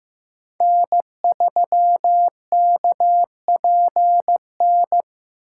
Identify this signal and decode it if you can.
Morse code, heard as a keyed tone, about 15 wpm, 700 Hz. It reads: N3KPN
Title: USA Call Signs - 5 Characters - 6x Character Spacing - 15wpm